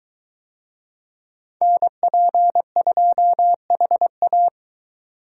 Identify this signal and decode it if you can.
Morse code, heard as a keyed tone, about 23 wpm, 700 Hz. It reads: NP2HA